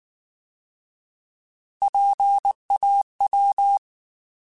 Morse code, heard as a keyed tone, about 19 wpm, 800 Hz. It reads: PAW
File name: Alphabetical / Alphabetical mp3s / Ps @18